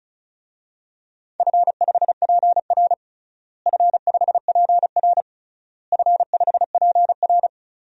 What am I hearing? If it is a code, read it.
Morse code, heard as a keyed tone, about 36 wpm, 700 Hz. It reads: F5PR F5PR F5PR
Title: DXCC-100 Call Signs - 4 Characters - Repeated 3x - 35wpm